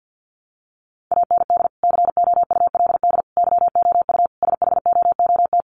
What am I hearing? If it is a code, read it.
Morse code, heard as a keyed tone, about 50 wpm, 700 Hz. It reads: AND COULD YOU SHOOT
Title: Sentences from Top 1200 Words 50wpm